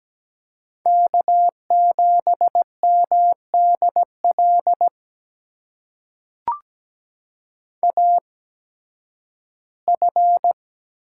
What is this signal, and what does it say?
Morse code, heard as a keyed tone, about 17 wpm, spaced as for 11 wpm, 700 Hz. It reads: K7MDL AF
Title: USA Call Signs - 5 Characters - 8x Character Spacing - 17wpm